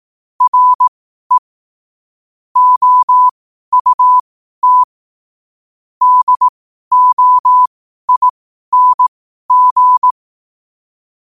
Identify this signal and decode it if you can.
Morse code, heard as a keyed tone, about 18 wpm, spaced as for 12 wpm, 1000 Hz. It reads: RE OUT DOING
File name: qotdfiles / 12wpm